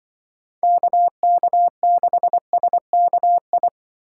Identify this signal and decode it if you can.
Morse code, heard as a keyed tone, about 24 wpm, 700 Hz. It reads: KK6SKI
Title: Speed Racing - USA Call Signs 20wpm